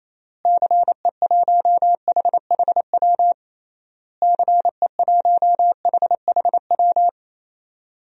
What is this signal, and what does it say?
Morse code, heard as a keyed tone, about 28 wpm, 700 Hz. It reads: CE1HHW CE1HHW
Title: DXCC-100 Call Signs - 6 Characters - Repeated 3x - 28wpm